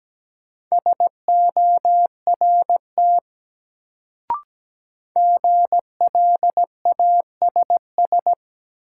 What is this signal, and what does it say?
Morse code, heard as a keyed tone, about 17 wpm, 700 Hz. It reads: SORT GLASS
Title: Sentences from Top 1000 Words - Review Part 3 - New Word Drill - 17wpm